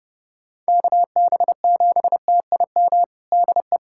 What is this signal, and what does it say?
Morse code, heard as a keyed tone, about 32 wpm, 700 Hz. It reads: KB7TIM DE
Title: USA Callsign DE Callsign - 6 Characters - 30wpm